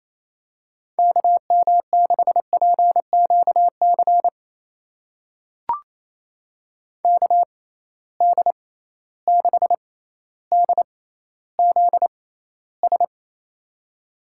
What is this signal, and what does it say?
Morse code, heard as a keyed tone, about 28 wpm, spaced as for 14 wpm, 700 Hz. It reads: KM6PQC KD6DZS
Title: USA Call Signs - 6 Characters - 6x Character Spacing - 28wpm